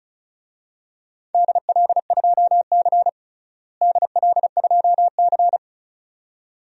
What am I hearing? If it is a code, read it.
Morse code, heard as a keyed tone, about 36 wpm, 700 Hz. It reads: DL2C DL2C
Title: DXCC-100 Call Signs - 4 Characters - Repeated 2x - 35wpm